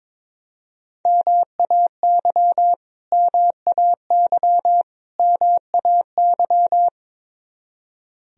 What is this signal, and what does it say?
Morse code, heard as a keyed tone, about 22 wpm, 700 Hz. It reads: MAY MAY MAY
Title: Top 100 Words Repeated 22wpm